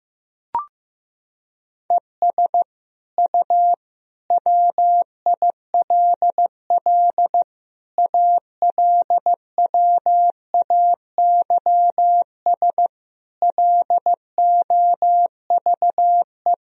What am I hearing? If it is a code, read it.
Morse code, heard as a keyed tone, about 15 wpm, 700 Hz. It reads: ES U WILL ALWAYS LOVE
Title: Sentences from Top 400 Words and Abbreviations 15wpm